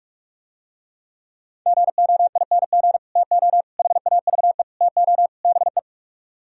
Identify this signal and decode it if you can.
Morse code, heard as a keyed tone, about 49 wpm, 700 Hz. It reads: GOING TO HAVE TO BE